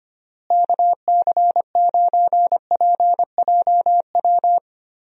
Morse code, heard as a keyed tone, about 25 wpm, 700 Hz. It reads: KC9PJW
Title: USA Call Signs - 6 Characters - Repeated 3x - 25wpm